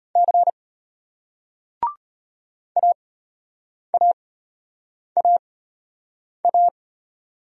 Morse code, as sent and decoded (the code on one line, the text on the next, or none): -.-. / .- .- .- .-
C AAAA